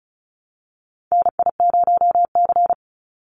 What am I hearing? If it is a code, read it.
Morse code, heard as a keyed tone, about 34 wpm, 700 Hz. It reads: NI0C
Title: Call Signs Repeated 2x - 35wpm